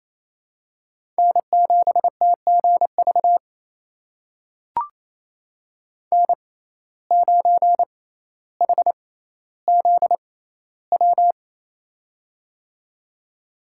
Morse code, heard as a keyed tone, about 28 wpm, spaced as for 14 wpm, 700 Hz. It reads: N7TGV N9HZW
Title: USA Call Signs - 5 Characters - 6x Character Spacing - 28wpm